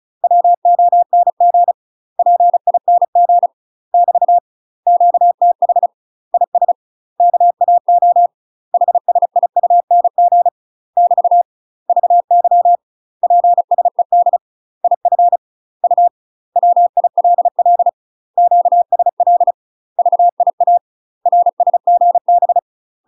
Morse code, heard as a keyed tone, about 35 wpm, 700 Hz. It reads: WONG PING <BT> QTH IS KAO HSIUNG <BT> VY PSED IF U WILL QSL VIA RSGB
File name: UK MASTER QSOs TRAINING / 35WPM / UK MASTER QSOs 0044